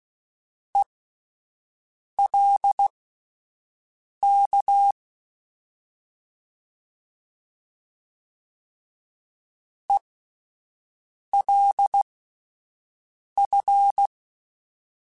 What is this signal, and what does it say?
Morse code, heard as a keyed tone, about 16 wpm, spaced as for 5 wpm, 800 Hz. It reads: ELK ELF
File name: Alphabetical / Alphabetical mp3s / Es @5